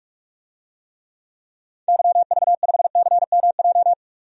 Morse code, heard as a keyed tone, about 46 wpm, 700 Hz. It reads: YV5CMJ
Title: DXCC-100 Call Signs - 6 Characters - 45wpm